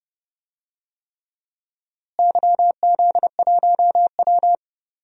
Morse code, heard as a keyed tone, about 30 wpm, 700 Hz. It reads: YZ1W
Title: DXCC-100 Call Signs - 4 Characters - Repeated 3x - 30wpm